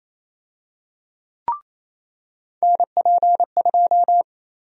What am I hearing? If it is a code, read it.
Morse code, heard as a keyed tone, about 28 wpm, 700 Hz. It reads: NP2
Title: USA Call Sign Prefixes - 28wpm